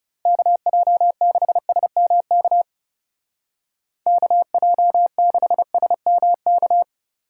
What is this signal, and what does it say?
Morse code, heard as a keyed tone, about 40 wpm, 700 Hz. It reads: KJ6SMK KJ6SMK